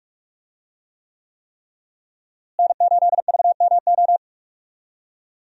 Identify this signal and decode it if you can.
Morse code, heard as a keyed tone, about 41 wpm, 700 Hz. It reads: N8VMO
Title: USA Call Signs - 5 Characters - Repeated 3x - 45wpm